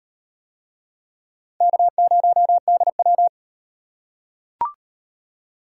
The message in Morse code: K0DW

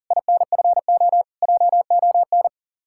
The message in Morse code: INFO JON

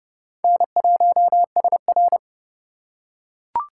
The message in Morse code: N1SR

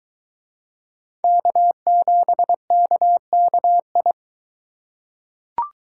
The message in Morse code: K7KKI

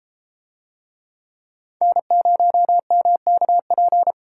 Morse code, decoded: N0MKP

